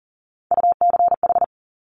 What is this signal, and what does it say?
Morse code, heard as a keyed tone, about 40 wpm, 700 Hz. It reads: UCH
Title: Top Three Letter Combinations in Words 40wpm